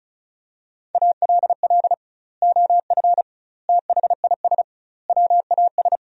Morse code, heard as a keyed tone, about 36 wpm, 700 Hz. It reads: ALL OF THIS WAS
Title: 2x Word-Spacing - Sentences from Top 500 Words - 35wpm